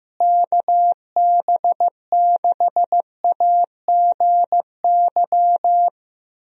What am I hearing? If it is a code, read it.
Morse code, heard as a keyed tone, about 15 wpm, 700 Hz. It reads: KB6AGY
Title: USA Call Signs - 6 Characters - Repeated 3x - 15wpm